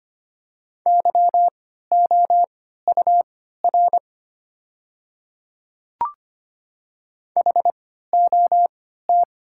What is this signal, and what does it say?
Morse code, heard as a keyed tone, about 25 wpm, spaced as for 14 wpm, 700 Hz. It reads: YOUR HOT
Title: 3x Character Spacing - Top 100 Words - 25wpm